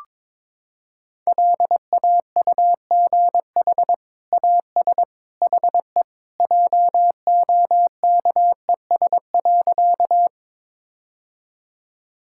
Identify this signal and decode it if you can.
Morse code, heard as a keyed tone, about 22 wpm, 700 Hz. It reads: LAUGH AS HE JOKES.
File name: practice / 22wpm / Sentences with 2-Word Phrases - 22wpm